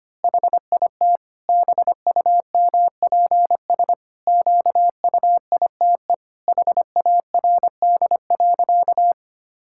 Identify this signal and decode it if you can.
Morse code, heard as a keyed tone, about 25 wpm, 700 Hz. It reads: HIT BUMPS QUITE HARD.
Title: Pangram Challenge v2 - 25wpm